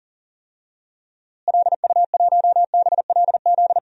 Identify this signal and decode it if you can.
Morse code, heard as a keyed tone, about 40 wpm, 700 Hz. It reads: LU1BLZ
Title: DXCC-100 Call Signs - 6 Characters - Repeated 3x - 40wpm